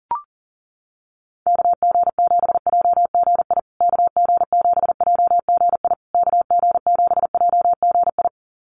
Morse code, heard as a keyed tone, about 40 wpm, 700 Hz. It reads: KG7JGI KG7JGI KG7JGI
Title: USA Call Signs Repeated 3x 40wpm